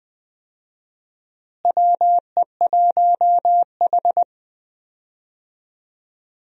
Morse code, heard as a keyed tone, about 20 wpm, 700 Hz. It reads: WE1H
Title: USA Call Signs - 4 Characters - Repeated 3x - 20wpm